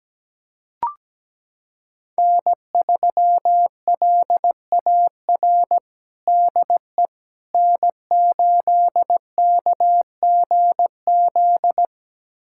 Morse code, heard as a keyed tone, about 17 wpm, 700 Hz. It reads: N3LAR DE N8KGZ
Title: USA Callsign DE Callsign - 5 Characters - 17wpm